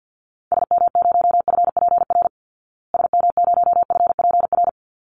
Morse code, heard as a keyed tone, about 50 wpm, 700 Hz. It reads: SM0FPR SM0FPR
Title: Call Signs Repeated 3x - 50wpm